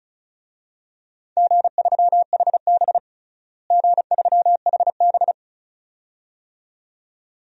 Morse code, heard as a keyed tone, about 34 wpm, 700 Hz. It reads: G3HB G3HB
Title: DXCC-100 Call Signs - 4 Characters - Repeated 2x - 35wpm